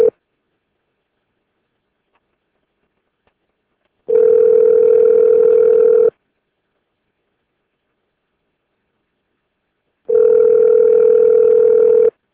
Ringback tone (North America), 440 + 480 Hz.